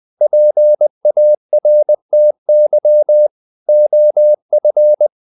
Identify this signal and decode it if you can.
Morse code, heard as a keyed tone, about 20 wpm, 600 Hz. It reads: PARTY OF
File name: the old man and the sea part 2 20wpm